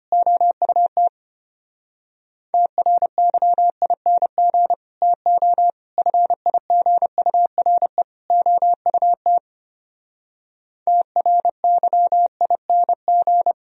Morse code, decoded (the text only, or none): OUT TRYING TO FIGURE OUT TRYING